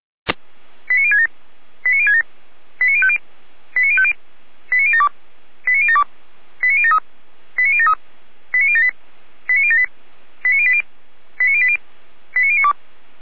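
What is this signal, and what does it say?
Five-tone selective call (ZVEI), heard as quick runs of five tones, tones of about 80 ms. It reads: 89067 89067 89050 89050 89072 89072 89073 89073 89078 89078 89080 89080 89092
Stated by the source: ZVEI five-tone selective call 89011 received via SDR.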